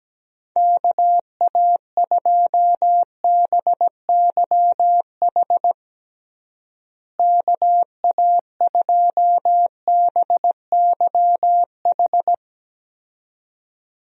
Morse code, heard as a keyed tone, about 17 wpm, 700 Hz. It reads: KA2BYH KA2BYH
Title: USA Call Signs - 6 Characters - Repeated 2x - 17wpm